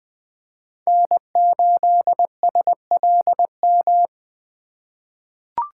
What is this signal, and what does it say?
Morse code, heard as a keyed tone, about 20 wpm, 700 Hz. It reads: N8SLM
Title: USA Call Signs - 5 Characters - 4x Character Spacing - 20wpm